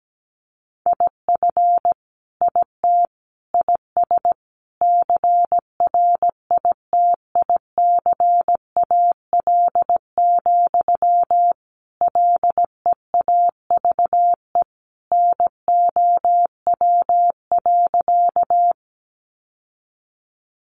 Morse code, heard as a keyed tone, about 17 wpm, 700 Hz. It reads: IF IT IS CRITICAL, LEAVE NOW.
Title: Sentences from Top 1100 Words 17wpm